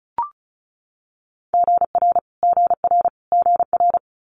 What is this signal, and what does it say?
Morse code, heard as a keyed tone, about 36 wpm, 700 Hz. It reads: GR GR GR
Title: Multi-letter Phonemes Repeated 3x 35wpm